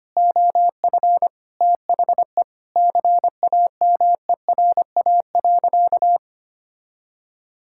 Morse code, heard as a keyed tone, about 25 wpm, 700 Hz. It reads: OF THE CAMERA.
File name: practice / 25wpm / Sentences with 3-Word Phrases - 25wpm